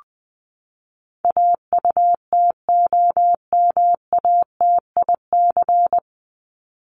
Morse code, heard as a keyed tone, about 20 wpm, 700 Hz. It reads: AUTOMATIC
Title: Top 1751-2000 Words 20wpm